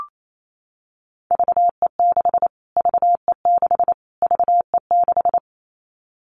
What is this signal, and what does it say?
Morse code, heard as a keyed tone, about 28 wpm, 700 Hz. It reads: VE6 VE6 VE6
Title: Callsign Prefixes for Top 25 DXCC - Repeated 3x - 28wpm